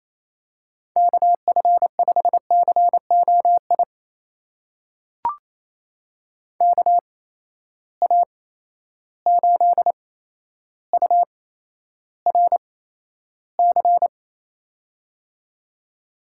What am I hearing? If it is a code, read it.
Morse code, heard as a keyed tone, about 28 wpm, spaced as for 12 wpm, 700 Hz. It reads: KF5COI KA8URC
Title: USA Call Signs - 6 Characters - 8x Character Spacing - 28wpm